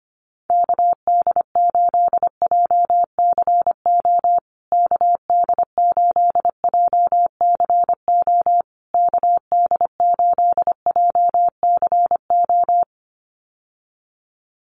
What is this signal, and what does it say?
Morse code, heard as a keyed tone, about 25 wpm, 700 Hz. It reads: KD8JCO KD8JCO KD8JCO
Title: USA Call Signs Repeated 3x 25wpm